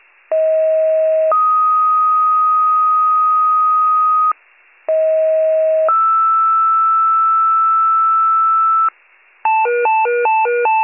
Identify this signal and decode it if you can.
Two-tone page, heard as a one-second tone, then a three-second tone, about 640 then 1250 Hz, about 640 then 1320 Hz. It is heard from 0.3 to 8.9 s.